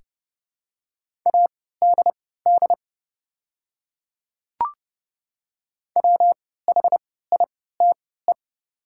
Morse code, heard as a keyed tone, about 30 wpm, spaced as for 17 wpm, 700 Hz. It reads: ADD WHITE
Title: 3x Character Spacing - Top 300 Words - 30wpm